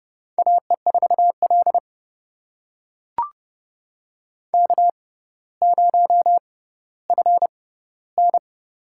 Morse code, heard as a keyed tone, about 29 wpm, spaced as for 15 wpm, 700 Hz. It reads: AE4L K0FN
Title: USA Call Signs - 4 Characters - 6x Character Spacing - 30wpm